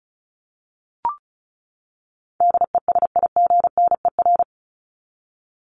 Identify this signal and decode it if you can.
Morse code, heard as a keyed tone, about 34 wpm, 700 Hz. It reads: DESIGNER
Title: Top 1501-1750 Words 35wpm